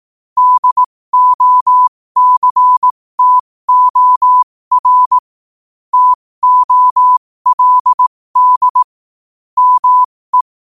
Morse code, heard as a keyed tone, about 18 wpm, spaced as for 15 wpm, 1000 Hz. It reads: DOCTOR TOLD ME
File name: qotdfiles / 15wpm